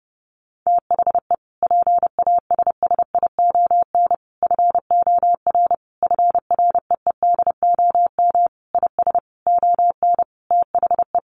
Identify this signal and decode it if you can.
Morse code, heard as a keyed tone, about 28 wpm, 700 Hz. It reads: THE PASSION FOR FREEDOM IS ON THE
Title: Sentences from Top 1300 Words 30wpm